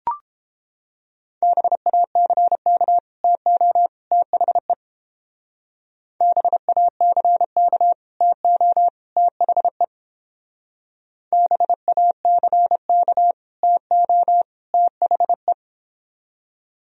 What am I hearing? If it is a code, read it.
Morse code, heard as a keyed tone, about 29 wpm, 700 Hz. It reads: BACK TO THE BACK TO THE BACK TO THE